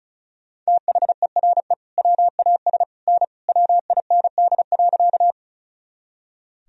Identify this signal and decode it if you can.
Morse code, heard as a keyed tone, about 34 wpm, 700 Hz. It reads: THERE WAS N WIND.
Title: Sentences from Top 400 Words and Abbreviations 35wpm